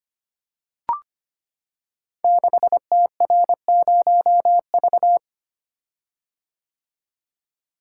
Morse code, heard as a keyed tone, about 25 wpm, 700 Hz. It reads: 6TR0V
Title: Code Groups - 5 Letter:Numbers - 25wpm